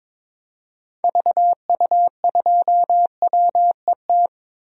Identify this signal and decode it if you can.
Morse code, heard as a keyed tone, about 22 wpm, 700 Hz. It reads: VU2WET
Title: DXCC-100 Call Signs - 6 Characters - 22wpm